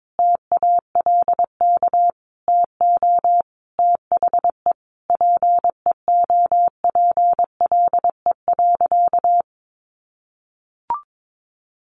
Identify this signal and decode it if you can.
Morse code, heard as a keyed tone, about 22 wpm, 700 Hz. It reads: TALK TO THE PEOPLE.
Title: Sentences from Top 400 Words 22wpm